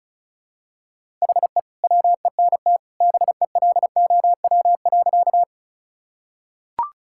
Morse code, heard as a keyed tone, about 36 wpm, 700 Hz. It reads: HE WENT BELOW.
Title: Sentences from Top 1000 Words - Review Part 5 - 35wpm